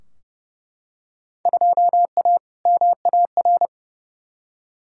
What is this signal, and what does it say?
Morse code, heard as a keyed tone, about 30 wpm, 700 Hz. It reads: 2A MAR
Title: Field Day Exchanges - 30wpm